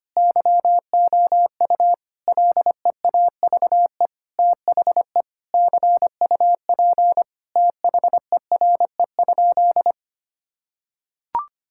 Morse code, heard as a keyed tone, about 25 wpm, 700 Hz. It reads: YOU LEAVE THE CUP THERE?
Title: Sentences from Top 1600 Words 25wpm